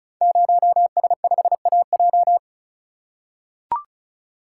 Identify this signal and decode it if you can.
Morse code, heard as a keyed tone, about 36 wpm, 690 Hz. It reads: 0S5AJ
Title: Code Groups - 5 Letter:Numbers - 35wpm